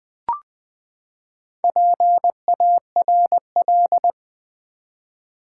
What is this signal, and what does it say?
Morse code, heard as a keyed tone, about 20 wpm, 700 Hz. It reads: PARL
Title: Build A Word - 4 Letters At A Time - 10-Letter Words - 20wpm